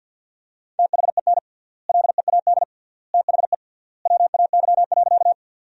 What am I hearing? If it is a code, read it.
Morse code, heard as a keyed tone, about 50 wpm, 700 Hz. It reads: THEN LEAD THE WAY.
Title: 3x Word-Spacing - Sentences from Top 500 Words - 50wpm